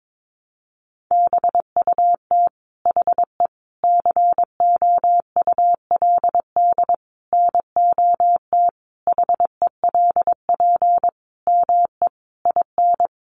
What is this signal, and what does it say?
Morse code, heard as a keyed tone, about 22 wpm, 700 Hz. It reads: BUT HE COULD NOT HELP ME IN
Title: Sentences from Top 600 Words 22wpm